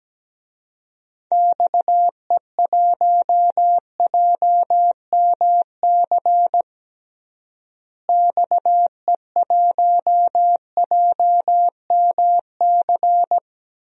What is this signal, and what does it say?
Morse code, heard as a keyed tone, about 17 wpm, 700 Hz. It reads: XE1JMC XE1JMC
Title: DXCC-100 Call Signs - 6 Characters - Repeated 2x - 17wpm